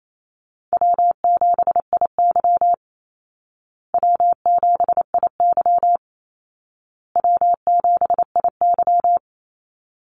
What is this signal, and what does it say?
Morse code, heard as a keyed tone, about 28 wpm, 700 Hz. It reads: W7IY W7IY W7IY
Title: Easy Call Signs Repeated 3x - 28wpm